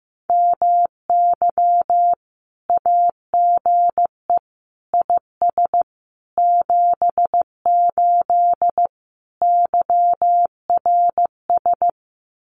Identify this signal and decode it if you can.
Morse code, heard as a keyed tone, about 15 wpm, 700 Hz. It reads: MY AGE IS 78 YRS
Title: Common QSO Phrases Encore 15wpm